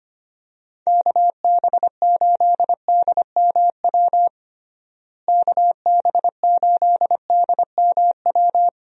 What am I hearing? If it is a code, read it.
Morse code, heard as a keyed tone, about 25 wpm, 700 Hz. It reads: KB8DMW KB8DMW
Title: USA Call Signs - 6 Characters - Repeated 2x - 25wpm